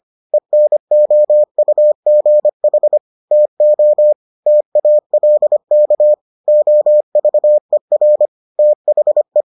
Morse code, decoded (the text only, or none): ENOUGH TO TALK OVER THE